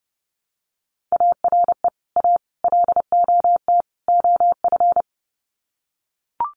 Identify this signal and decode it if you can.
Morse code, heard as a keyed tone, about 30 wpm, 700 Hz. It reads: ARE A LOT OF
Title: Sets of 4 Words 30wpm